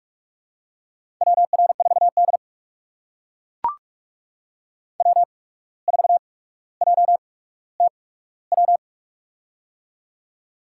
Morse code, heard as a keyed tone, about 44 wpm, spaced as for 18 wpm, 700 Hz. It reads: WR4D W4JTW
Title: USA Call Signs - 8x Character Spacing - 45wpm